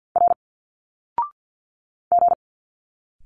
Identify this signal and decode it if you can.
Morse code, heard as a keyed tone, about 50 wpm, 700 Hz. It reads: R G
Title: Single Letters 50wpm